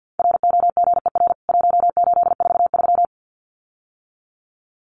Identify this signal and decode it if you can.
Morse code, heard as a keyed tone, about 48 wpm, 700 Hz. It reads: ROGER 1843
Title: CWT Exchange 50wpm